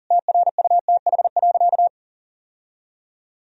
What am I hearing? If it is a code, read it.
Morse code, heard as a keyed tone, about 37 wpm, 700 Hz. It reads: TRUTH.
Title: Sentences from Top 1000 Words - Review Part 1 - 40wpm